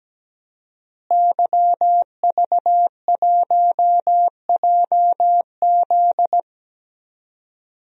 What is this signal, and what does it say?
Morse code, heard as a keyed tone, about 17 wpm, 700 Hz. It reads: YV1JZ
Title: DXCC-100 Call Signs - 5 Characters - Repeated 3x - 17wpm